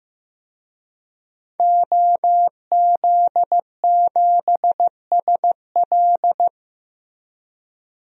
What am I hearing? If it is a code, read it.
Morse code, heard as a keyed tone, about 15 wpm, 700 Hz. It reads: OZ7SL
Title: DXCC-100 Call Signs - 5 Characters - Repeated 2x - 15wpm